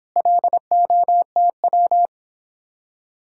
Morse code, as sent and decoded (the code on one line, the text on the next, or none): .-.. --- - .--
LOTW